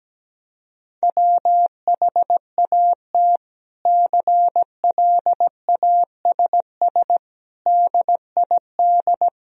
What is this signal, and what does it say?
Morse code, heard as a keyed tone, about 17 wpm, 700 Hz. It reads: WHAT CLASS DID
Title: Sentences from Top 1000 Words - Review Part 2 - 17wpm